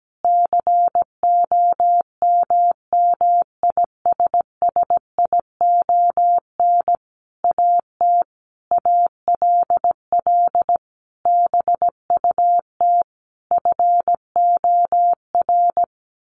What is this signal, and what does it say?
Morse code, heard as a keyed tone, about 17 wpm, 700 Hz. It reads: COMMISSION AT ALL BUT FOR
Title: Sentences from Top 1400 Words 17wpm